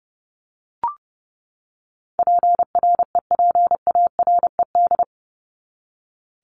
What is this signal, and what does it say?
Morse code, heard as a keyed tone, about 30 wpm, 700 Hz. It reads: PREPARED
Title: ED Suffix - Top 100 Words with Base Words 30wpm